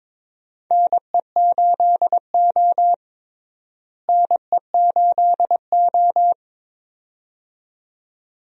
Morse code, heard as a keyed tone, about 23 wpm, 700 Hz. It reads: NE8O NE8O
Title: USA Call Signs - 4 Characters - Repeated 2x - 22wpm